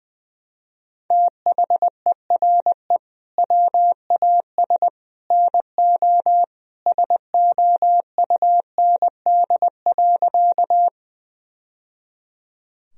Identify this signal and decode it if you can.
Morse code, heard as a keyed tone, about 20 wpm, 700 Hz. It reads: THERE WAS NO SOUND.
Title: Sentences from Top 1000 Words - Review Part 1 - 20wpm